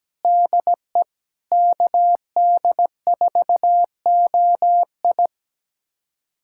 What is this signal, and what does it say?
Morse code, heard as a keyed tone, about 17 wpm, 700 Hz. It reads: DE KD4OI
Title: USA Callsign DE Callsign 17wpm